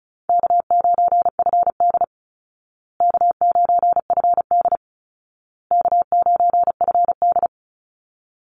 Morse code, heard as a keyed tone, about 34 wpm, 700 Hz. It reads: K9FD K9FD K9FD
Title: Easy Call Signs Repeated 3x - 35wpm